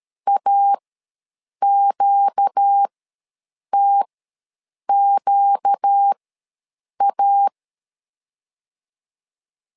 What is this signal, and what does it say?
Morse code, heard as a keyed tone, about 13 wpm, spaced as for 7 wpm, 800 Hz. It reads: AQTQA
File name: LEZIONI / 4 Q ws 12wpm 1912